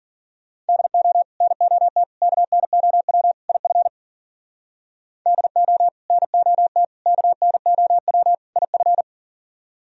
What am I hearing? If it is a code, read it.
Morse code, heard as a keyed tone, about 50 wpm, 700 Hz. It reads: DO NOT KNOW IF DO NOT KNOW IF